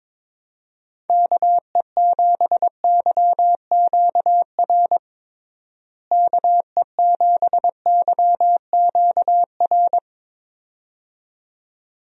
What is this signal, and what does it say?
Morse code, heard as a keyed tone, about 22 wpm, 700 Hz. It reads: KE7YQR KE7YQR
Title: USA Call Signs - 6 Characters - Repeated 2x - 22wpm